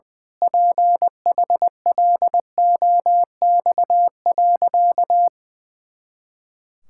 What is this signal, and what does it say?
Morse code, heard as a keyed tone, about 20 wpm, 700 Hz. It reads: PHLOX.
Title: Pangram Challenge v2 - 20wpm